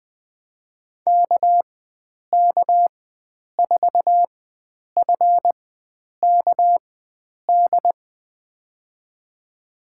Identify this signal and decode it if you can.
Morse code, heard as a keyed tone, about 20 wpm, spaced as for 9 wpm, 700 Hz. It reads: KK4FKD
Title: USA Call Signs - 6 Characters - 4x Character Spacing - 20wpm